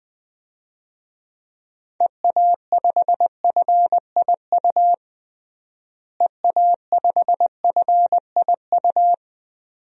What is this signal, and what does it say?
Morse code, heard as a keyed tone, about 20 wpm, 700 Hz. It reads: EA5FIU EA5FIU
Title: DXCC-100 Call Signs - 6 Characters - Repeated 2x - 20wpm